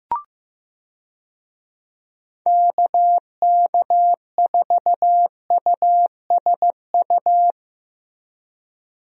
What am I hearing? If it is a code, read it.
Morse code, heard as a keyed tone, about 15 wpm, 700 Hz. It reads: KK4USU
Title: USA Call Signs - 6 Characters - Repeated 3x - 15wpm